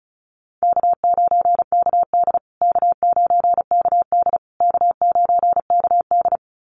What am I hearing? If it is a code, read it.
Morse code, heard as a keyed tone, about 34 wpm, 700 Hz. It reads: K9KD K9KD K9KD
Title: USA Call Signs Repeated 3x 35wpm